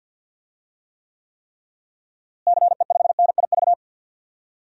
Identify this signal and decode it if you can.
Morse code, heard as a keyed tone, about 49 wpm, 700 Hz. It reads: CE5NIV
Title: DXCC-100 Call Signs - 6 Characters - Repeated 3x - 50wpm